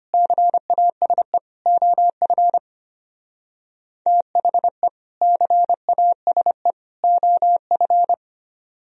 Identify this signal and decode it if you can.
Morse code, heard as a keyed tone, about 31 wpm, 700 Hz. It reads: CASE OF THE CASE OF
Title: Speed-Racing - Sets of 3 Words 25wpm